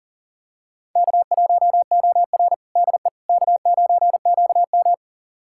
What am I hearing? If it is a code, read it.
Morse code, heard as a keyed tone, about 40 wpm, 700 Hz. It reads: K1OR DE K9QM
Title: USA Callsign DE Callsign - 4 Characters - 40wpm